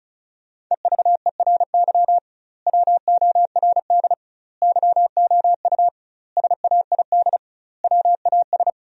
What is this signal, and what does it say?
Morse code, heard as a keyed tone, about 36 wpm, 700 Hz. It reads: EVERY WORD YOU SAID WAS
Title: 2x Word-Spacing - Sentences from Top 400 Words - 35wpm